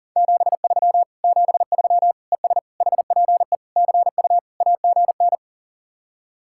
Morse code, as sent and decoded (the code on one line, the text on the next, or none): --... ...-- / --... ...-- / . ... / .... .--. / -.-. ..- / .- --. -.
73 73 ES HPE CU AGN